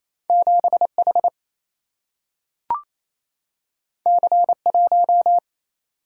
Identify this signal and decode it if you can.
Morse code, heard as a keyed tone, about 28 wpm, 700 Hz. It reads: 7 H C 1
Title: Single-Letter Single-Number 28wpm